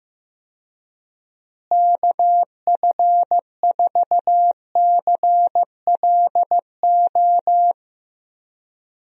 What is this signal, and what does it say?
Morse code, heard as a keyed tone, about 15 wpm, 700 Hz. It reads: KF4CLO
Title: USA Call Signs - 6 Characters - Repeated 3x - 15wpm